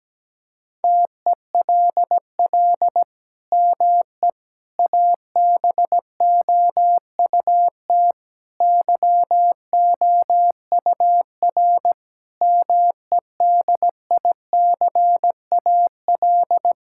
Morse code, heard as a keyed tone, about 17 wpm, 700 Hz. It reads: TELL ME ABOUT YOUR MEDICAL
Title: Sentences from Top 1000 Words - Review Part 5 - 17wpm